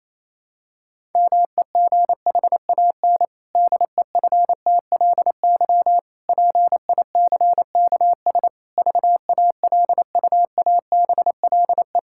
Morse code, heard as a keyed tone, about 28 wpm, 700 Hz. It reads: MEGHAN DEFTLY PICKS VALUABLE
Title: Pangram Challenge v2 - 28wpm